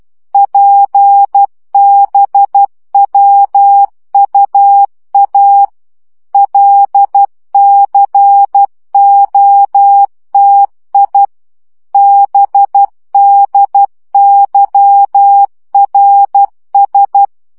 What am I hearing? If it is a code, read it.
Morse code, heard as a keyed tone, about 12 wpm, 800 Hz. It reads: PBWUA LCOTI BDYRS
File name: morze vizsga 2011-3 10